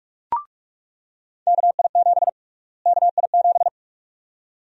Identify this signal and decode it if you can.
Morse code, heard as a keyed tone, about 42 wpm, 700 Hz. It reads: KI7 KI7
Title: USA Call Sign Prefixes - Repeated 2x - 45wpm